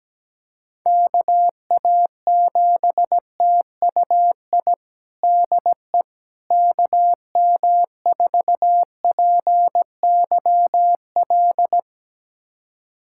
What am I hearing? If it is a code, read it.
Morse code, heard as a keyed tone, about 17 wpm, 700 Hz. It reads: KA7TUI DE KM4PYL
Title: USA Callsign DE Callsign 17wpm